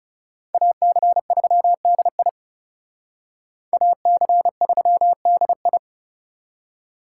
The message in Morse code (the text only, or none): AC3DI AC3DI